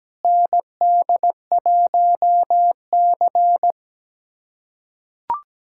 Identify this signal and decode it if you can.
Morse code, heard as a keyed tone, about 17 wpm, 700 Hz. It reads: ND1C
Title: USA Callsign DE Callsign - 4 Characters - 17wpm